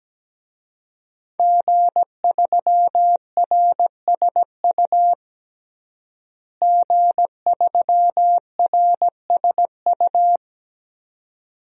Morse code, heard as a keyed tone, about 17 wpm, 700 Hz. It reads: G3RSU G3RSU
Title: DXCC-100 Call Signs - 5 Characters - Repeated 2x - 17wpm